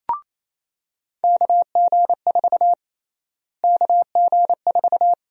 Morse code, heard as a keyed tone, about 28 wpm, 700 Hz. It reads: KG4 KG4
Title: USA Call Sign Prefixes - Repeated 2x - 28wpm